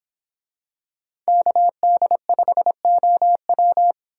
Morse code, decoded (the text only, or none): KD5OW